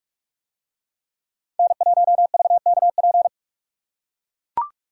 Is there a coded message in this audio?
Yes: Morse code from 1.6 to 3.3 s, heard as a keyed tone, about 44 wpm, 700 Hz. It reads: N1VKP